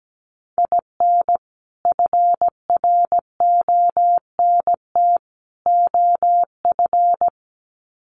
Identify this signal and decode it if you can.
Morse code, heard as a keyed tone, about 17 wpm, 700 Hz. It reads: IN FRONT OF
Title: Sets of 3 Words 17wpm